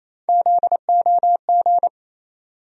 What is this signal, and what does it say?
Morse code, heard as a keyed tone, about 28 wpm, 700 Hz. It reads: ZOG
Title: USA Call Sign Postfixes - 3 Letters 28wpm